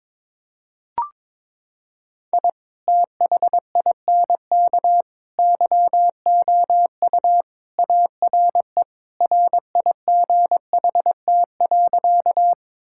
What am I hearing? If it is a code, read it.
Morse code, heard as a keyed tone, about 22 wpm, 700 Hz. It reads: I THINK YOU ARE RIGHT.
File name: practice / 22wpm / Sentences with 3-Word Phrases - 22wpm